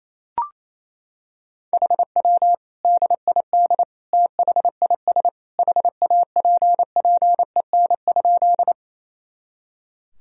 Morse code, heard as a keyed tone, about 28 wpm, 700 Hz. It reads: HW DID THIS HAPPEN?
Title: Sentences from Top 400 Words and Abbreviations 28wpm